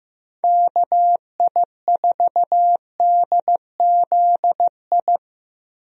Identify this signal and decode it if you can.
Morse code, heard as a keyed tone, about 16 wpm, 700 Hz. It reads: KI4DZI
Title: USA Call Signs - 6 Characters - 15wpm